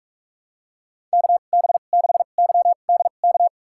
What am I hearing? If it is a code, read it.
Morse code, heard as a keyed tone, about 42 wpm, spaced as for 31 wpm, 700 Hz. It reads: KB6YDK
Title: USA Call Signs - 6 Characters - 2x Character Spacing - 45wpm